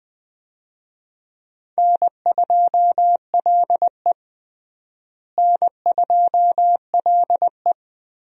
Morse code, heard as a keyed tone, about 20 wpm, 700 Hz. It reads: N2LE N2LE
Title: USA Call Signs - 4 Characters - Repeated 3x - 20wpm